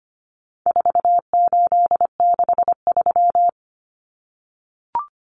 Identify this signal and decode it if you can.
Morse code, heard as a keyed tone, about 25 wpm, 700 Hz. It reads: 4863
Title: Common Cut Numbers - 4-Digit 25wpm